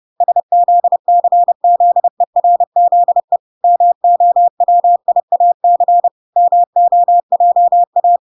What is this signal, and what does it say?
Morse code, heard as a keyed tone, about 30 wpm, 700 Hz. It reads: SZCZERZE MOWIAC MOJA